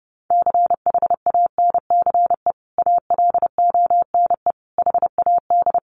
Morse code, heard as a keyed tone, about 28 wpm, 700 Hz. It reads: CHANCE ALONE HAD
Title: Sentences from Top 1000 Words 30wpm